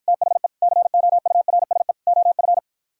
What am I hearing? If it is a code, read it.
Morse code, heard as a keyed tone, about 52 wpm, 700 Hz. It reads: THE COURSE OF